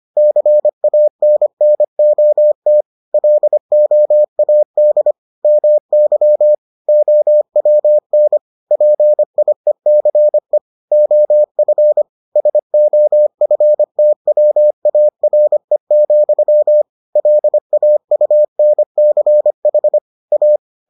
Morse code, heard as a keyed tone, about 25 wpm, 600 Hz. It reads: CANNOT LOAD MY OWN PIECE OF SOFTWARE, LAUNCH A